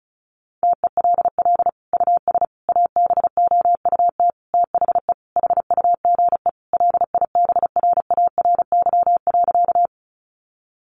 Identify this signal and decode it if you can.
Morse code, heard as a keyed tone, about 34 wpm, 700 Hz. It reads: TELL US ABOUT THE HUGE LIBRARY.
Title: Sentences from Top 1100 Words 35wpm